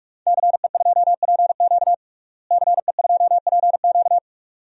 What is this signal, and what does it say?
Morse code, heard as a keyed tone, about 46 wpm, 700 Hz. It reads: CE2PQ CE2PQ
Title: DXCC-100 Call Signs - 5 Characters - Repeated 2x - 45wpm